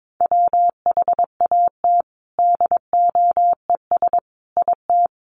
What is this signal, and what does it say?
Morse code, heard as a keyed tone, about 22 wpm, 700 Hz. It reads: WHAT DOES IT
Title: Sets of 5 Words Encore2 22wpm